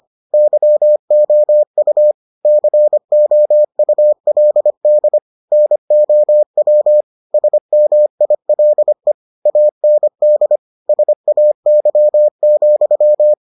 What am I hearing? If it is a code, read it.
Morse code, heard as a keyed tone, about 25 wpm, 600 Hz. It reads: YOU COULD NOW SMILE AND SAY,